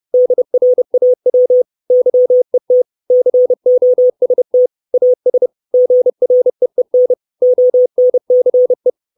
Morse code, heard as a keyed tone, about 30 wpm, 500 Hz. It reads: DRAW YET COST AS GREEN ONCE
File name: 632 Words 30WPM